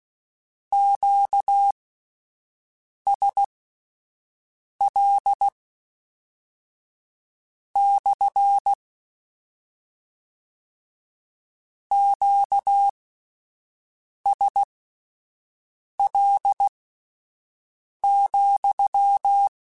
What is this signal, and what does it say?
Morse code, heard as a keyed tone, about 16 wpm, spaced as for 5 wpm, 800 Hz. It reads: QSL / QSL,
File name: Q Calls / Q Calls-mp3s / Q CALLS SPSL @10